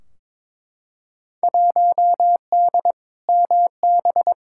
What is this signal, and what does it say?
Morse code, heard as a keyed tone, about 22 wpm, 700 Hz. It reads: 1D MB
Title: Field Day Exchanges Encore - 22wpm